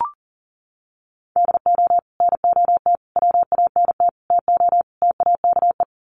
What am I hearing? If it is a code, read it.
Morse code, heard as a keyed tone, about 40 wpm, 700 Hz. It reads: DO NOT WANT TO TAKE
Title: Sets of 5 Words Encore2 40wpm